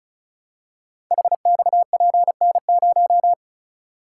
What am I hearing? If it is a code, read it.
Morse code, heard as a keyed tone, about 33 wpm, 700 Hz. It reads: HXPN0
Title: Code Groups - 5 Letter:Numbers - 35wpm